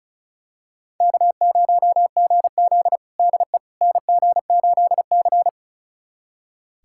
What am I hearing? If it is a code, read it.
Morse code, heard as a keyed tone, about 36 wpm, 700 Hz. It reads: K0GZ DE NG8C
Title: USA Callsign DE Callsign - 4 Characters - 35wpm